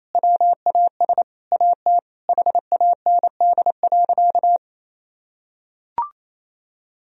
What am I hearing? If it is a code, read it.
Morse code, heard as a keyed tone, about 28 wpm, 700 Hz. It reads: WAS AT HAND.
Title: Sentences from Top 1000 Words - Review Part 2 - 28wpm